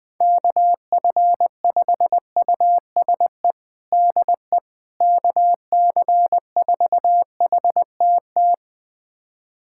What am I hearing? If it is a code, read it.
Morse code, heard as a keyed tone, about 20 wpm, 700 Hz. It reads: KF5USE DE KC4HTT
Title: USA Callsign DE Callsign 20wpm